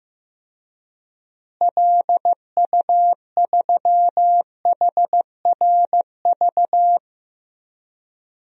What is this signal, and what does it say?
Morse code, heard as a keyed tone, about 16 wpm, 700 Hz. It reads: LU3HRV
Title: DXCC-100 Call Signs - 6 Characters - Repeated 3x - 15wpm